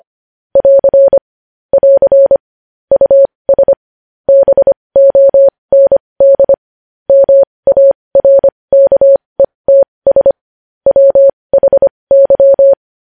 Morse code, heard as a keyed tone, about 25 wpm, spaced as for 20 wpm, 550 Hz. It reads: <AR> <AR> US BOND MARKETS WHY